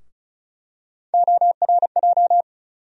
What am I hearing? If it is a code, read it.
Morse code, heard as a keyed tone, about 36 wpm, 700 Hz. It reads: ORJ
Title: USA Call Sign Postfixes - 3 Letters 35wpm